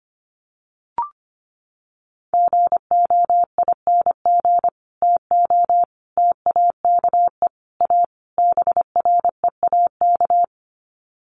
Morse code, heard as a keyed tone, about 25 wpm, 700 Hz. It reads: GOING TO TAKE A BREAK
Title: Sets of 5 Words 25wpm